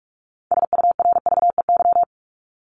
Morse code, heard as a keyed tone, about 44 wpm, 700 Hz. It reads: SURVEY
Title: Top 1501-1750 Words 45wpm